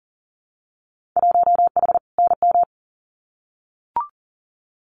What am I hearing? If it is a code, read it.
Morse code, heard as a keyed tone, about 40 wpm, 700 Hz. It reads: 1H NM
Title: Winter Field Day Exchanges 40wpm